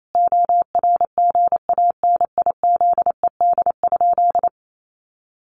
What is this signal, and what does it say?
Morse code, heard as a keyed tone, about 28 wpm, 700 Hz. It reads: ORGANIZED?
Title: Sentences from Top 1300 Words 28wpm